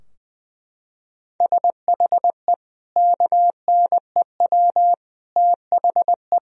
Morse code, heard as a keyed tone, about 20 wpm, 700 Hz. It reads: SHE KNEW THE
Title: Sentences from Top 1000 Words - Review Part 4 - 20wpm